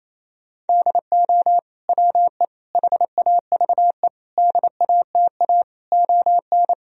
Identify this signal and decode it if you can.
Morse code, heard as a keyed tone, about 27 wpm, 700 Hz. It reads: DO WE HAVE DATA ON
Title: Sentences from Top 1000 Words - Review Part 3 - 28wpm